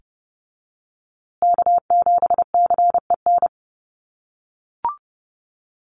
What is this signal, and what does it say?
Morse code, heard as a keyed tone, about 30 wpm, 700 Hz. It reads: K7CEN
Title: USA Call Signs Repeated 3x 30wpm